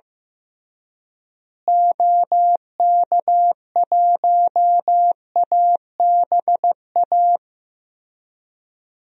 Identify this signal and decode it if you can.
Morse code, heard as a keyed tone, about 15 wpm, 700 Hz. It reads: OK1ABA
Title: DXCC-100 Call Signs - 6 Characters - Repeated 3x - 15wpm